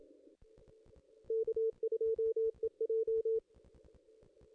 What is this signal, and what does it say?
Morse code, heard as a keyed tone, about 27 wpm, 450 Hz. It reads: K2EJ
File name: contests / ss-cw-k5tr-op-n5rz -1600